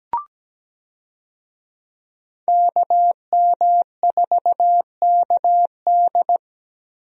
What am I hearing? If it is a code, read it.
Morse code, heard as a keyed tone, about 17 wpm, 700 Hz. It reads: KM4KD
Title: USA Call Signs - 5 Characters - Repeated 3x - 17wpm